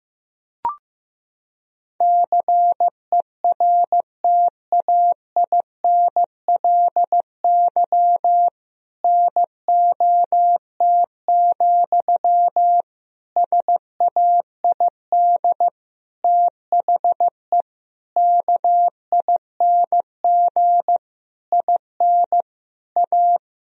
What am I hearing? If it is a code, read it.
Morse code, heard as a keyed tone, about 15 wpm, 700 Hz. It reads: CERTAINLY NOT, SAID THE KING IN A
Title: Sentences from Top 1000 Words - Review Part 1 - 15wpm